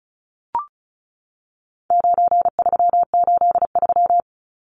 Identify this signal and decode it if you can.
Morse code, heard as a keyed tone, about 34 wpm, 700 Hz. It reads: 9383
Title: Numbers - 4-Digit 35wpm